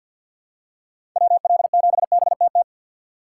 Morse code, heard as a keyed tone, about 52 wpm, 700 Hz. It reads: WL7BTT
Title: USA Call Signs - 4x Character Spacing - 50wpm